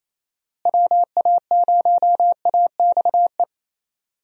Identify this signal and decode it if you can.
Morse code, heard as a keyed tone, about 28 wpm, 700 Hz. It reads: WA0AXE
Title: USA Call Signs - 6 Characters - Repeated 2x - 28wpm